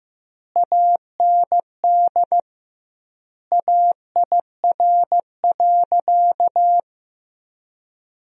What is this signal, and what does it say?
Morse code, heard as a keyed tone, about 16 wpm, 700 Hz. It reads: AND AIR.
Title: 2x Word-Spacing - Sentences from Top 400 Words - 15wpm